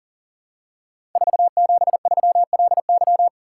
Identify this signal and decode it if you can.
Morse code, heard as a keyed tone, about 41 wpm, 700 Hz. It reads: 473LY
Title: Code Groups - 5 Letter:Numbers - 40wpm